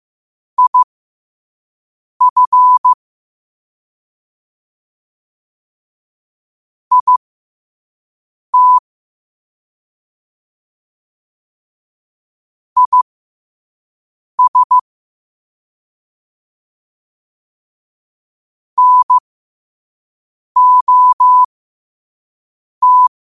Morse code, heard as a keyed tone, about 15 wpm, spaced as for 5 wpm, 1000 Hz. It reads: IF IT IS NOT
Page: New quotes every day in morse code at 5 Words per minute.